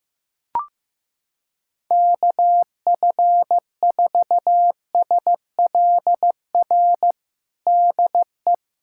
Morse code, heard as a keyed tone, about 15 wpm, 700 Hz. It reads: KF4SLR DE
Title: USA Callsign DE Callsign - 6 Characters - 15wpm